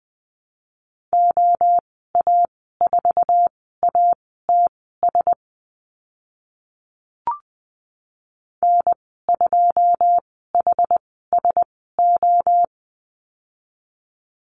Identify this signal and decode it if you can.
Morse code, heard as a keyed tone, about 20 wpm, spaced as for 14 wpm, 700 Hz. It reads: OA4ATS N2HSO
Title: 082 Lesson 14g - Number 2 - Call Signs 20wpm